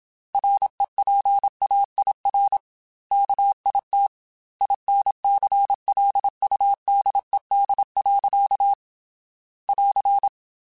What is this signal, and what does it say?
Morse code, heard as a keyed tone, about 26 wpm, 800 Hz. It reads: REPAIR KIT INCLUDED. <AR>
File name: Quotes / Quotes-mp3s / Airtight @25